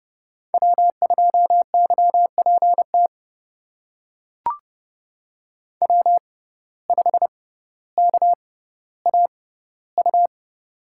Morse code, heard as a keyed tone, about 30 wpm, spaced as for 15 wpm, 700 Hz. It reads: W2YPT W5KAU
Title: USA Call Signs - 5 Characters - 6x Character Spacing - 30wpm